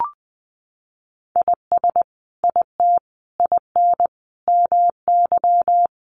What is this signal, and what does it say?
Morse code, heard as a keyed tone, about 20 wpm, 700 Hz. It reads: IS IT IN MY
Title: Sentences from Top 1200 Words 20wpm